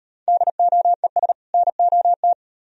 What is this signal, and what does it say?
Morse code, heard as a keyed tone, about 36 wpm, 700 Hz. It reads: DOES NOT